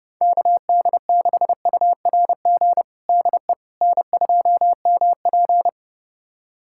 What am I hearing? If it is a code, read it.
Morse code, heard as a keyed tone, about 30 wpm, 700 Hz. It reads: KD6URG DE N2MP
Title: USA Callsign DE Callsign 30wpm